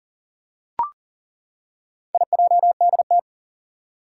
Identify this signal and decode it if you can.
Morse code, heard as a keyed tone, about 38 wpm, 690 Hz. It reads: IJDT
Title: Four Challenging Letter-Number Combinations 40wpm